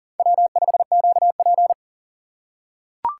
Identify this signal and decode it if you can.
Morse code, heard as a keyed tone, about 37 wpm, 700 Hz. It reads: W5QP
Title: K1USN SST QSO Practice - 40wpm